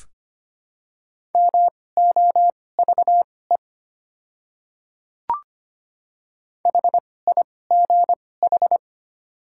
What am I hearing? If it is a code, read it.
Morse code, heard as a keyed tone, about 25 wpm, spaced as for 18 wpm, 700 Hz. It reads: MOVE HIGH
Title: 2x Character Spacing - Top 200 Words - 25wpm